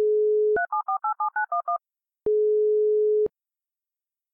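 DTMF, heard as telephone touch tones, tones of about 110 ms. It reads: 37487911
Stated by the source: DTMF dialing tones from a dial-up modem connection.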